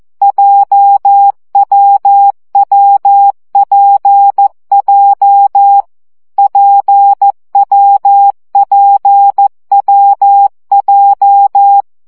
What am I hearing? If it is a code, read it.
Morse code, heard as a keyed tone, about 14 wpm, 800 Hz. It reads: JWWPJ PWPWJ
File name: CW gyakorlas w p j 12 baud